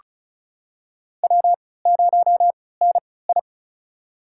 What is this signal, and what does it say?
Morse code, heard as a keyed tone, about 34 wpm, spaced as for 20 wpm, 700 Hz. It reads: W0NI
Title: USA Call Signs - 3x Character Spacing - 35wpm